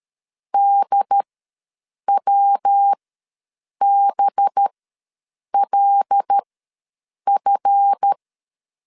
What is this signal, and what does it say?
Morse code, heard as a keyed tone, about 13 wpm, spaced as for 7 wpm, 800 Hz. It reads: DWBLF
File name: LEZIONI / 7 GF9Z / 7 GF ws 12wpm 3140